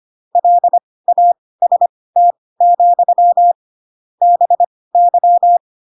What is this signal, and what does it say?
Morse code, heard as a keyed tone, about 25 wpm, spaced as for 18 wpm, 700 Hz. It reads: LAST, BY
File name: Pinocchio ch3 12 25 ik1hge